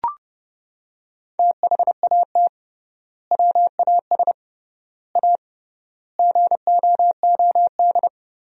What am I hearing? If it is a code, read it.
Morse code, heard as a keyed tone, about 32 wpm, 700 Hz. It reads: THAT WAS A GOOD
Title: 3x Word-Spacing - Sentences from Top 200 Words - 30wpm